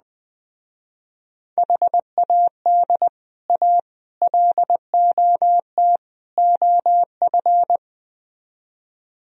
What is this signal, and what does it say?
Morse code, heard as a keyed tone, about 20 wpm, 700 Hz. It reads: HAD A LOT OF
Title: Speed-Racing - Sets of 4 Words 17wpm